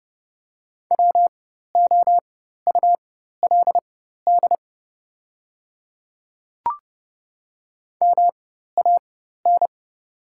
Morse code, heard as a keyed tone, about 29 wpm, spaced as for 14 wpm, 700 Hz. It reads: WOULD MAN
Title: 4x Character Spacing - Top 300 Words - 30wpm